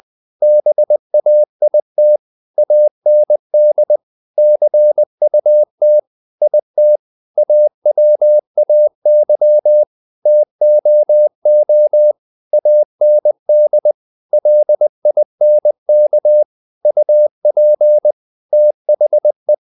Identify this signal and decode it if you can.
Morse code, heard as a keyed tone, about 20 wpm, 600 Hz. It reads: BAIT AND CUT IT AWAY TOO AND LINK UP THE
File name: the old man and the sea / the old man and the sea part 1 20wpm